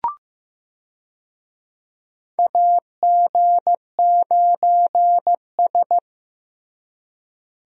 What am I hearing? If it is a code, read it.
Morse code, heard as a keyed tone, about 16 wpm, 700 Hz. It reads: AG9S
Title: USA Call Signs - 4 Characters - Repeated 2x - 15wpm